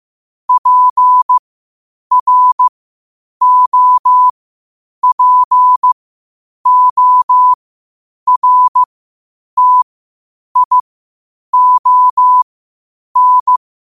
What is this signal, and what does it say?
Morse code, heard as a keyed tone, about 15 wpm, spaced as for 8 wpm, 1000 Hz. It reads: PROPORTION
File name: qotdfiles / 8wpm